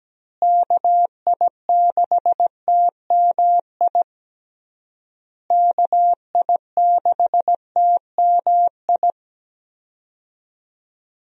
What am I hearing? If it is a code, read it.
Morse code, heard as a keyed tone, about 17 wpm, 700 Hz. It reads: KI6TMI KI6TMI
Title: USA Call Signs - 6 Characters - Repeated 3x - 17wpm